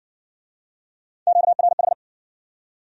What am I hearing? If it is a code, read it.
Morse code, heard as a keyed tone, about 59 wpm, 700 Hz. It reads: QRH